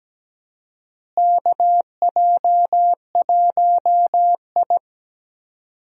Morse code, heard as a keyed tone, about 17 wpm, 700 Hz. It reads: KJ1I
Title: USA Call Signs - 4 Characters - Repeated 3x - 17wpm